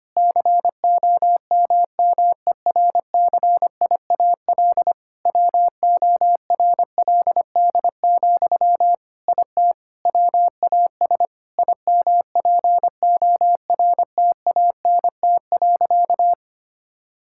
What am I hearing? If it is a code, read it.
Morse code, heard as a keyed tone, about 25 wpm, 700 Hz. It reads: COMMERCIAL WORLD, IT WAS IMPORTANT.
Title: Sentences from Top 1000 Words - Review Part 5 - 25wpm